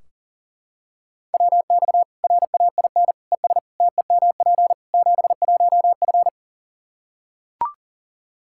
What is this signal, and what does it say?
Morse code, heard as a keyed tone, about 40 wpm, 700 Hz. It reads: WX RAIN ES TEMP 71F
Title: K1USN SST QSO Practice 40wpm